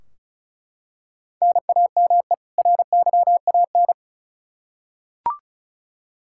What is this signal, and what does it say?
Morse code, heard as a keyed tone, about 36 wpm, 700 Hz. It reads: NAME RYAN
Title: K1USN SST QSO Practice 35wpm